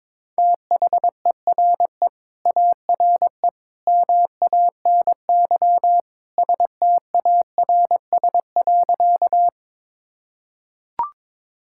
Morse code, heard as a keyed tone, about 22 wpm, 700 Hz. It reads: THERE ARE MANY STARS.
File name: practice / 22wpm / Sentences with 2-Word Phrases - 22wpm